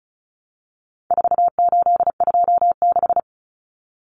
Morse code, heard as a keyed tone, about 34 wpm, 700 Hz. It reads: 4826
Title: Numbers - 4-Digit 35wpm